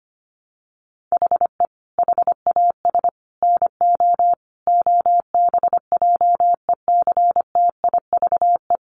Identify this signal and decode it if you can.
Morse code, heard as a keyed tone, about 25 wpm, 700 Hz. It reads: HE HAS NO OBJECTIVE
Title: Sentences from Top 1200 Words 25wpm